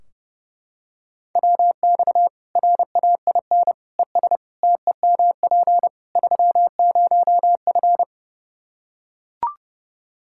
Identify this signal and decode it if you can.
Morse code, heard as a keyed tone, about 32 wpm, 700 Hz. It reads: WX RAIN ES TEMP 30F
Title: K1USN SST QSO Practice 30wpm